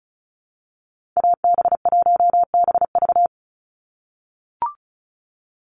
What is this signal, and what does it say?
Morse code, heard as a keyed tone, about 34 wpm, 700 Hz. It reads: AB1BV
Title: USA Call Signs 35wpm